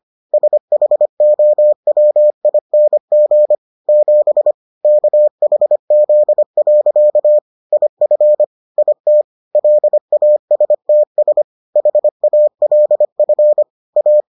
Morse code, heard as a keyed tone, about 25 wpm, 600 Hz. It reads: SHOWING 7 KHZ. IF IT LASTS HALF A